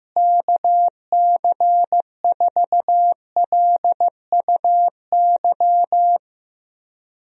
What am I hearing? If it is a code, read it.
Morse code, heard as a keyed tone, about 15 wpm, 700 Hz. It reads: KC4LUY
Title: USA Call Signs - 6 Characters - Repeated 3x - 15wpm